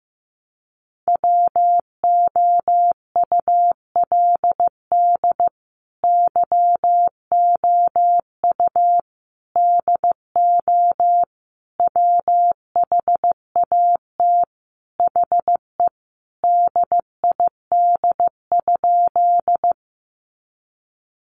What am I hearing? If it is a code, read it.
Morse code, heard as a keyed tone, about 15 wpm, 700 Hz. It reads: WOULD YOU DO WHAT HE DID?
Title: Sentences from Top 100 Words 15wpm